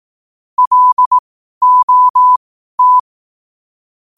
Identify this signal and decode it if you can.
Morse code, heard as a keyed tone, about 18 wpm, spaced as for 12 wpm, 1000 Hz. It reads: LOT
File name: qotdfiles / 12wpm